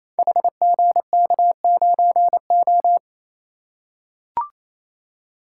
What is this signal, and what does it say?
Morse code, heard as a keyed tone, about 28 wpm, 700 Hz. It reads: HGK9O
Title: Code Groups - 5 Letter:Numbers - 28wpm